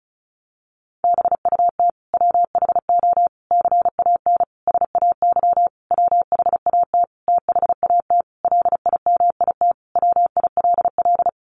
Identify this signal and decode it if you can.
Morse code, heard as a keyed tone, about 34 wpm, 700 Hz. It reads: BUT WHO CAN SAY WHAT THAT LIMIT WILL
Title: Sentences from Top 1500 Words 35wpm